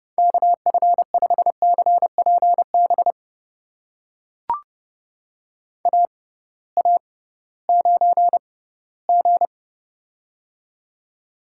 Morse code, heard as a keyed tone, about 29 wpm, spaced as for 17 wpm, 700 Hz. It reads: KF5CPB AA9G
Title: USA Call Signs - 6x Character Spacing - 30wpm